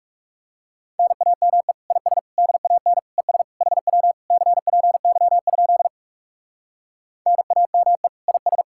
Morse code, heard as a keyed tone, about 40 wpm, 700 Hz. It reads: NAME IS DAN ES HW CPY? NAME IS